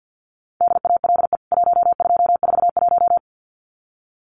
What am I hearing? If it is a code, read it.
Morse code, heard as a keyed tone, about 48 wpm, 700 Hz. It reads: DALE 1241
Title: CWT Exchange 50wpm